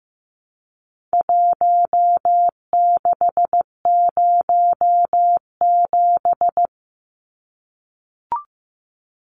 Morse code, heard as a keyed tone, about 15 wpm, 700 Hz. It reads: 1607
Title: Numbers - Contest Style - 4-Digit 15wpm